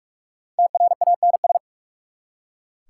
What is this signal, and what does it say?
Morse code, heard as a keyed tone, about 46 wpm, 690 Hz. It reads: TRANS
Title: Prefixes and Suffixes 45wpm